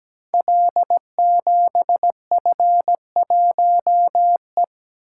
Morse code, heard as a keyed tone, about 17 wpm, 700 Hz. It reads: L7F1E
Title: Code Groups - 5 Letter:Numbers - 17wpm